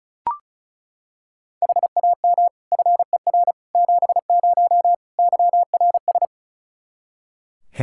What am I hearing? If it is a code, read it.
Morse code, heard as a keyed tone, about 34 wpm, 700 Hz. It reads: HAM FER 70 YRS
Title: Practice QSO - Exchange 3 - More Info - 35wpm